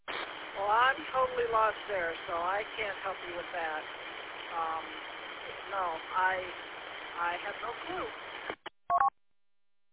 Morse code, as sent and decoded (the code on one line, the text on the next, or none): none